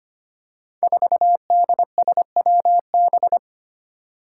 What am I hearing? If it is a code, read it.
Morse code, heard as a keyed tone, about 25 wpm, 700 Hz. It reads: 4DSWB
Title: Five Challenging Letter-Number Combinations 25wpm